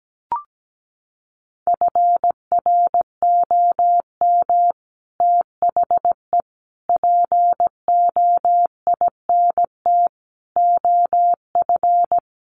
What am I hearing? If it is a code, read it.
Morse code, heard as a keyed tone, about 17 wpm, 700 Hz. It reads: FROM THE POINT OF
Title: Sets of 5 Words 17wpm